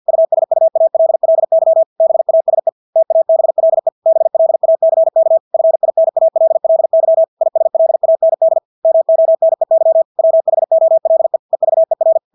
Morse code, heard as a keyed tone, about 50 wpm, 650 Hz. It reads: USUALLY BASE TABLE BLACK FINALLY ISLAND MONEY WHOLE EVER
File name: Engelske ord 50 wpm